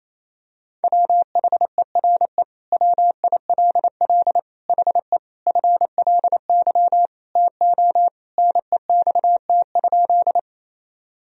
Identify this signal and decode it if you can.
Morse code, heard as a keyed tone, about 28 wpm, 700 Hz. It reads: WHERE WILL HE FLY TO NEXT?
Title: Sentences from Top 1000 Words - Review Part 2 - 28wpm